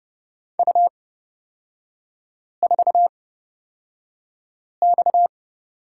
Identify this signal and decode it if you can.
Morse code, heard as a keyed tone, about 30 wpm, spaced as for 5 wpm, 700 Hz. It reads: U4X
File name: Lesson 37e - Letter x - Three Characters 30wpm